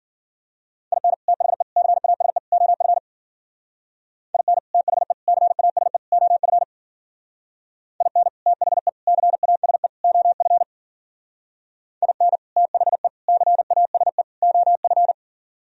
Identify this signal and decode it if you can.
Morse code, heard as a keyed tone, about 52 wpm, 700 Hz. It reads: IN THE CASE OF IN THE CASE OF IN THE CASE OF IN THE CASE OF